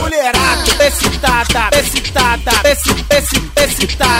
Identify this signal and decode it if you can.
There is no code.